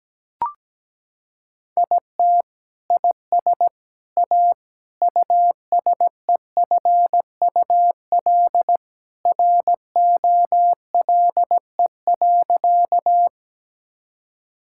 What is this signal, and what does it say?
Morse code, heard as a keyed tone, about 17 wpm, 700 Hz. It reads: IT IS A USEFUL ROLE.
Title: Sentences from Top 1000 Words - Review Part 6 - 17wpm